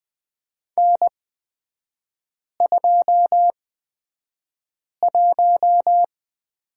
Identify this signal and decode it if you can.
Morse code, heard as a keyed tone, about 20 wpm, spaced as for 5 wpm, 700 Hz. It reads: N21
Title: 224 Lesson 36e - BT - Three Characters 20wpm